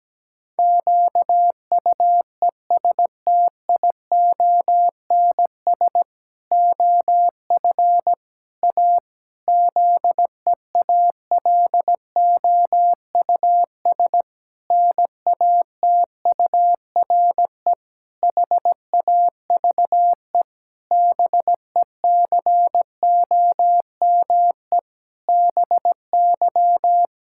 Morse code, heard as a keyed tone, about 17 wpm, 700 Hz. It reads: QUESTIONS OF A ZEALOUS NATURE HAVE BECOME BY
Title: Pangram Challenge v2 - 17wpm